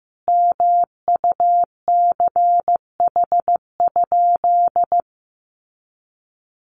Morse code, heard as a keyed tone, about 15 wpm, 700 Hz. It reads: MUCH?
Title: Sentences from Top 1200 Words 15wpm